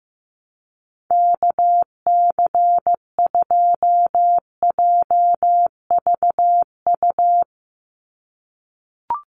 Morse code, heard as a keyed tone, about 15 wpm, 700 Hz. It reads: KC2JVU
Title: USA Call Signs 15wpm